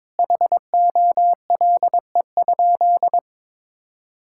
Morse code, heard as a keyed tone, about 22 wpm, 700 Hz. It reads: HOLE?
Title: Sentences from Top 1600 Words 22wpm